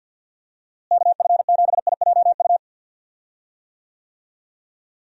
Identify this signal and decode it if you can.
Morse code, heard as a keyed tone, about 50 wpm, 700 Hz. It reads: KF7IJU
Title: USA Call Signs - 6 Characters - Repeated 3x - 50wpm